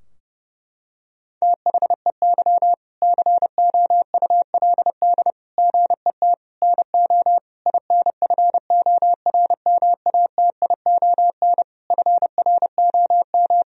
Morse code, heard as a keyed tone, about 30 wpm, 700 Hz. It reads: THEY COULD GET NO INFORMATION FROM
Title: Sentences from Top 1000 Words - Review Part 3 - 30wpm